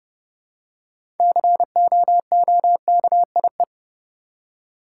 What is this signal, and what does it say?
Morse code, heard as a keyed tone, about 30 wpm, 700 Hz. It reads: COOKIE
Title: Six Letter Words - 30wpm